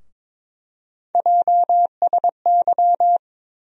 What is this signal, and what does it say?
Morse code, heard as a keyed tone, about 22 wpm, 700 Hz. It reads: JSY
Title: USA Call Sign Postfixes - 3 Letters 22wpm